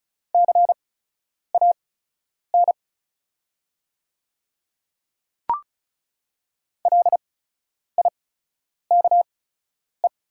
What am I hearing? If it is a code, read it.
Morse code, heard as a keyed tone, about 36 wpm, spaced as for 10 wpm, 700 Hz. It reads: CAN LIKE
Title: 8x Character Spacing - Top 100 Words - 35wpm